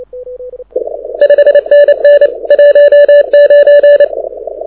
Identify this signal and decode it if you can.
Morse code, heard as a keyed tone, about 29 wpm, 580 Hz. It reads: 5NN 19
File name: audio zawody / IARU 2015 CW SN7Q